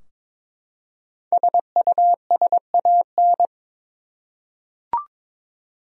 Morse code, heard as a keyed tone, about 22 wpm, 700 Hz. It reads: SUSAN
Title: USA Common Names - Max 5 Letters - 22wpm